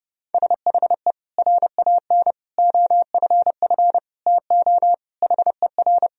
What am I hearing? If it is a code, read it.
Morse code, heard as a keyed tone, about 32 wpm, 700 Hz. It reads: SHE RAN OFF TO HER
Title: Sentences from Top 1000 Words - Review Part 3 - 30wpm